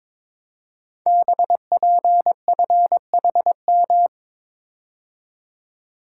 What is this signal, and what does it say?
Morse code, heard as a keyed tone, about 22 wpm, 700 Hz. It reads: BPFHM
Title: Code Groups - 5 Letters - 22wpm